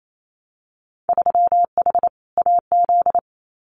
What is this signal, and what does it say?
Morse code, heard as a keyed tone, about 28 wpm, 700 Hz. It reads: 3H AZ
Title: Winter Field Day Exchanges Encore 28wpm